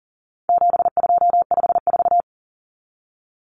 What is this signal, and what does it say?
Morse code, heard as a keyed tone, about 40 wpm, 700 Hz. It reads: 7254
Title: Common Cut Numbers - 4-Digit 40wpm